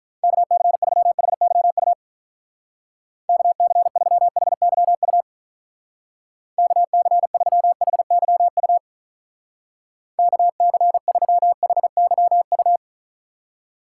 Morse code, heard as a keyed tone, about 45 wpm, 700 Hz. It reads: KC3HYU KC3HYU KC3HYU KC3HYU